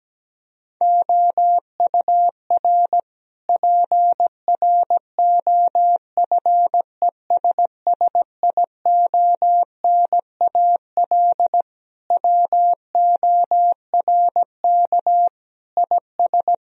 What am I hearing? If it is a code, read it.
Morse code, heard as a keyed tone, about 17 wpm, 700 Hz. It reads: OUR PROFESSIONAL WORK IS
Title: Sentences from Top 1000 Words - Review Part 4 - 17wpm